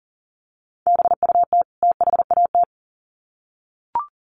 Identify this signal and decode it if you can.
Morse code, heard as a keyed tone, about 40 wpm, 700 Hz. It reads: BUT THAT
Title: Sets of 2 Words 40wpm